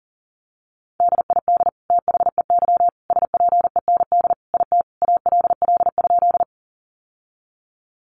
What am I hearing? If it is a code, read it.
Morse code, heard as a keyed tone, about 40 wpm, 700 Hz. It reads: DID THEY SPEND IT ALL?
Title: Sentences from Top 800 Words 40wpm